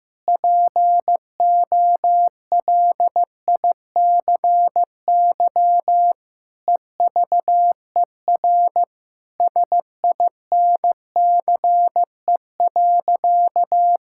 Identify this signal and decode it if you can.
Morse code, heard as a keyed tone, about 15 wpm, 700 Hz. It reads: POLICY EVER SINCE.
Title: Sentences from Top 1000 Words - Review Part 5 - 15wpm